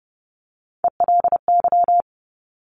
Morse code, heard as a keyed tone, about 30 wpm, 690 Hz. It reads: ELY
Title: Top Three Letter Combinations in Words 30wpm